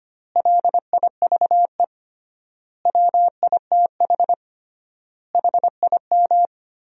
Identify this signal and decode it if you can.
Morse code, heard as a keyed tone, about 25 wpm, 700 Hz. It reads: LIVE WITH HIM
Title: 3x Word-Spacing - Sentences from Top 500 Words - 25wpm